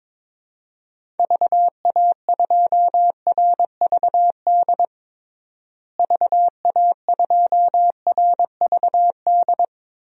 Morse code, heard as a keyed tone, about 22 wpm, 700 Hz. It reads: VA2RVD VA2RVD
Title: DXCC-100 Call Signs - 6 Characters - Repeated 2x - 22wpm